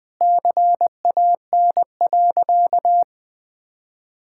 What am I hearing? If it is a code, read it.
Morse code, heard as a keyed tone, about 20 wpm, 700 Hz. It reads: CAN.
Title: Sentences from Top 1000 Words - Review Part 5 - 20wpm